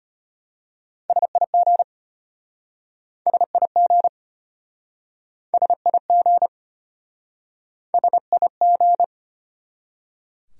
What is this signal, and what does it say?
Morse code, heard as a keyed tone, about 34 wpm, 700 Hz. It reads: SIG SIG SIG SIG